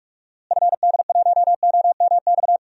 Morse code, heard as a keyed tone, about 46 wpm, 700 Hz. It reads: FD1OMX
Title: DXCC-100 Call Signs - 6 Characters - Repeated 2x - 45wpm